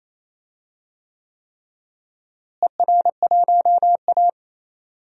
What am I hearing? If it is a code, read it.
Morse code, heard as a keyed tone, about 28 wpm, 700 Hz. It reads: ER1A
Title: DXCC-100 Call Signs - 4 Characters - Repeated 3x - 28wpm